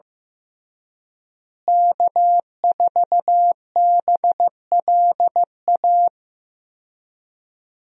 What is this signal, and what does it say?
Morse code, heard as a keyed tone, about 15 wpm, 700 Hz. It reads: K4BLA
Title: USA Call Signs - 5 Characters - Repeated 3x - 15wpm